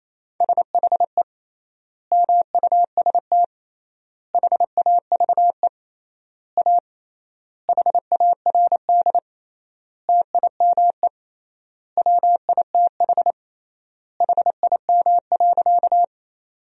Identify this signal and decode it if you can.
Morse code, heard as a keyed tone, about 28 wpm, 700 Hz. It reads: SHE MUST HAVE A HARD TIME WITH HIM.
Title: 3x Word-Spacing - Sentences from Top 300 Words - 28wpm